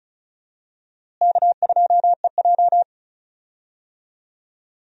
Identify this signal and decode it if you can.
Morse code, heard as a keyed tone, about 36 wpm, 700 Hz. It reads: K2EJ
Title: USA Call Signs - 4 Characters - Repeated 3x - 35wpm